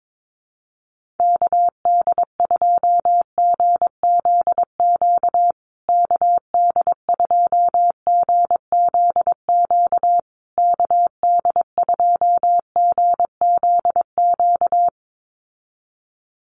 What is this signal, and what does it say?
Morse code, heard as a keyed tone, about 22 wpm, 700 Hz. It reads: KD2GZQ KD2GZQ KD2GZQ
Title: USA Call Signs Repeated 3x 22wpm